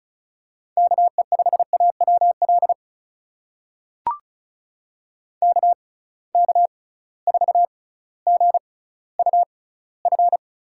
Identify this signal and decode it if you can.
Morse code, heard as a keyed tone, about 36 wpm, spaced as for 18 wpm, 700 Hz. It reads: KE5AWL KK4GUF
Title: USA Call Signs - 6 Characters - 6x Character Spacing - 35wpm